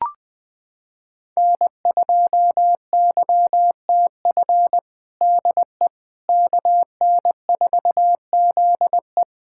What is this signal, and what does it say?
Morse code, heard as a keyed tone, about 20 wpm, 700 Hz. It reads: N2YTF DE KN4ZE
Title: USA Callsign DE Callsign - 5 Characters - 20wpm